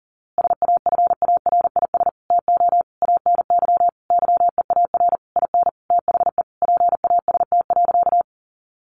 Morse code, heard as a keyed tone, about 40 wpm, 700 Hz. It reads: SAFARIS TO ANY YEAR IN THE PAST.
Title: A Sound Of Thunder 40wpm